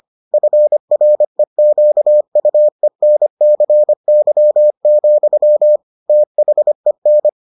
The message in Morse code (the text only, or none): FREQUENCY, THEN